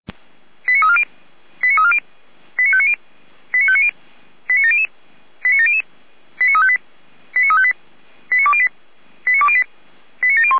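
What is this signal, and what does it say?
Five-tone selective call (ZVEI), heard as quick runs of five tones, tones of about 70 ms. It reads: 89340 89340 89590 89590 89700 89700 89358 89358 89208 89208 89701
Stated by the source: ZVEI five-tone selective call received on an SDR.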